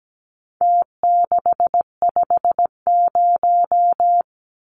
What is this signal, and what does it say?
Morse code, heard as a keyed tone, about 17 wpm, 700 Hz. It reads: T650
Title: Numbers - Contest Style - 4-Digit 17wpm